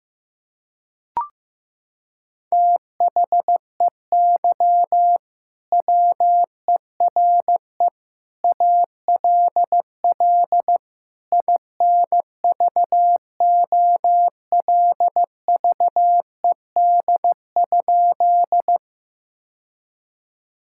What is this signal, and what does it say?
Morse code, heard as a keyed tone, about 14 wpm, 700 Hz. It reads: THEY WERE ALL INVOLVED?
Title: Sentences from Top 1000 Words - Review Part 5 - 15wpm